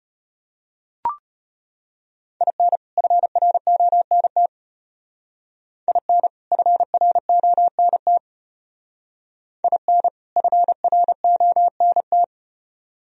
Morse code, heard as a keyed tone, about 34 wpm, 700 Hz. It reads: IN FRONT IN FRONT IN FRONT